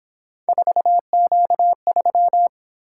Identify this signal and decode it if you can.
Morse code, heard as a keyed tone, about 26 wpm, 700 Hz. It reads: 4Q3
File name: Speed-Racing - Three Letter-Number Combinations - 22wpm